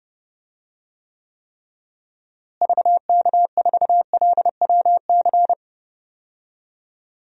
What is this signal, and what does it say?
Morse code, heard as a keyed tone, about 30 wpm, 700 Hz. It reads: VK4LWC
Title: DXCC-100 Call Signs - 6 Characters - Repeated 3x - 30wpm